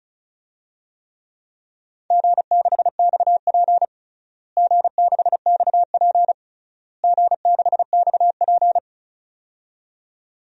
Morse code, heard as a keyed tone, about 36 wpm, 700 Hz. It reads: G6XP G6XP G6XP
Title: DXCC-100 Call Signs - 4 Characters - Repeated 3x - 35wpm